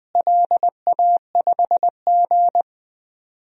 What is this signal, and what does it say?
Morse code, heard as a keyed tone, about 20 wpm, 700 Hz. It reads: LA5G
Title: DXCC-100 Call Signs - 4 Characters - Repeated 3x - 20wpm